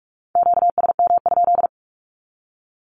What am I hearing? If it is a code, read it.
Morse code, heard as a keyed tone, about 44 wpm, 700 Hz. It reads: QSM?
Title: Q-Codes 45wpm